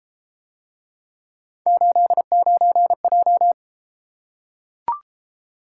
Morse code, heard as a keyed tone, about 32 wpm, 700 Hz. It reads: 89J